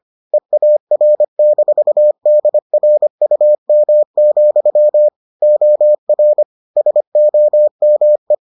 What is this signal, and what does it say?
Morse code, heard as a keyed tone, about 25 wpm, 600 Hz. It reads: EAR-DRUM, OR SOME